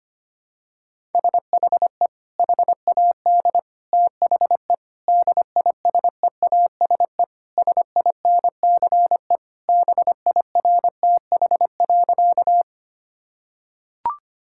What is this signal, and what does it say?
Morse code, heard as a keyed tone, about 25 wpm, 700 Hz. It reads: SHE HAD THE DISEASE SINCE BIRTH.
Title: Sentences from Top 1000 Words - Review Part 6 - 25wpm